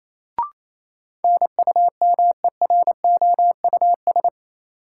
Morse code, heard as a keyed tone, about 28 wpm, 700 Hz. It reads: NUMEROUS
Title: Sentences from Top 1400 Words - New Word Drill - 28wpm